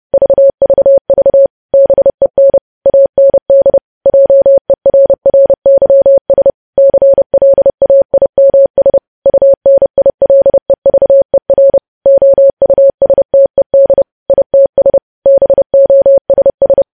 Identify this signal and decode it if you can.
Morse code, heard as a keyed tone, about 30 wpm, 550 Hz. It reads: VVV BEN AND JERRYS CLAIMS UNILEVER OUSTED ITS BOSS
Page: news headlines in morse code at 30 wpm